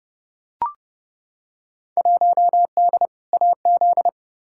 Morse code, heard as a keyed tone, about 30 wpm, 700 Hz. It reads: 1D AZ
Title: Field Day Exchanges - 30wpm